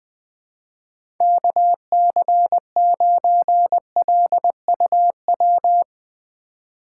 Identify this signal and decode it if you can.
Morse code, heard as a keyed tone, about 21 wpm, 700 Hz. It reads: KC9LUW
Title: USA Call Signs - 6 Characters - 2x Character Spacing - 20wpm